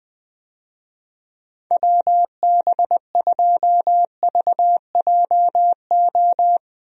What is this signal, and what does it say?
Morse code, heard as a keyed tone, about 20 wpm, 700 Hz. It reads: WB2VJO
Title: USA Call Signs - 6 Characters - Repeated 2x - 20wpm